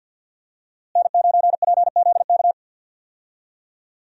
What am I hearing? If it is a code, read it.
Morse code, heard as a keyed tone, about 52 wpm, 700 Hz. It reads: N9PZK
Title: USA Call Signs - 5 Characters - Repeated 3x - 50wpm